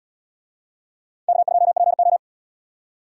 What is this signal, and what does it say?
Morse code, heard as a keyed tone, about 77 wpm, 700 Hz. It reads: K2PO